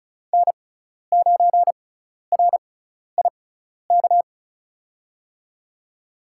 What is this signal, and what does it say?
Morse code, heard as a keyed tone, about 36 wpm, spaced as for 12 wpm, 700 Hz. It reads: N9RIK
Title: USA Call Signs - 6x Character Spacing - 35wpm